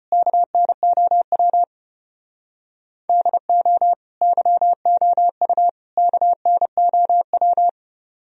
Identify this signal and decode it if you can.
Morse code, heard as a keyed tone, about 31 wpm, 700 Hz. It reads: KNOW DO YOU KNOW
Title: Speed-Racing - Sets of 3 Words 25wpm